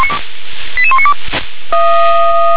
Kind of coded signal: five-tone selective call